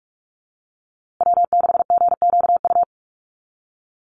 Morse code, heard as a keyed tone, about 44 wpm, 700 Hz. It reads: W6GQU
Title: USA Call Signs Repeated 2x 45wpm